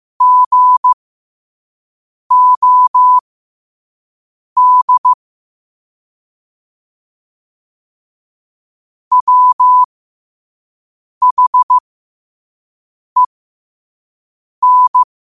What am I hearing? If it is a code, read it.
Morse code, heard as a keyed tone, about 15 wpm, spaced as for 5 wpm, 1000 Hz. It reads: GOD WHEN
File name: qotdfiles / 5wpm